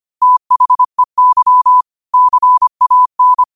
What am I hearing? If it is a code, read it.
Morse code, heard as a keyed tone, about 25 wpm, 1000 Hz. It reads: THEY CAN
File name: qotdfiles / 25wpm